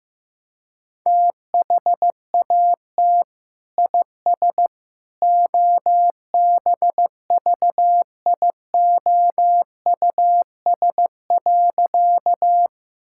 Morse code, heard as a keyed tone, about 15 wpm, 700 Hz. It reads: THAT IS OBVIOUS.
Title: Sentences from Top 1600 Words 15wpm